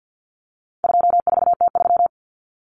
Morse code, heard as a keyed tone, about 48 wpm, 700 Hz. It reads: 24T3
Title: Common Cut Numbers - 4-Digit 50wpm